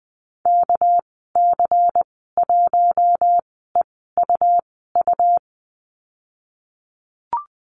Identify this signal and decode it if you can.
Morse code, heard as a keyed tone, about 20 wpm, spaced as for 15 wpm, 700 Hz. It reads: KC1EUU
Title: 238 Lesson 38g - Letter q - Call Signs 20wpm